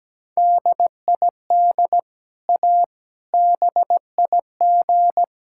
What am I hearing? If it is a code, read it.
Morse code, heard as a keyed tone, about 17 wpm, 700 Hz. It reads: DID A BIG
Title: Sentences from Top 1000 Words - Review Part 6 - 17wpm